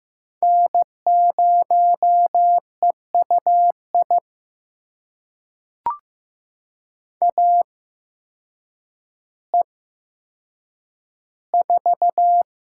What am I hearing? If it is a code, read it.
Morse code, heard as a keyed tone, about 15 wpm, spaced as for 8 wpm, 700 Hz. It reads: N0EUI AE4
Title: USA Call Signs - 5 Characters - 8x Character Spacing - 15wpm